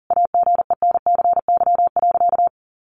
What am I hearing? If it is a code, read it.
Morse code, heard as a keyed tone, about 40 wpm, 700 Hz. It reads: AGENCY.
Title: Sentences from Top 1300 Words 40wpm